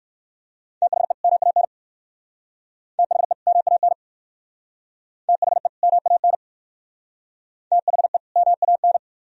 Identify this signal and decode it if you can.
Morse code, heard as a keyed tone, about 57 wpm, 700 Hz. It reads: THE MAN THE MAN THE MAN THE MAN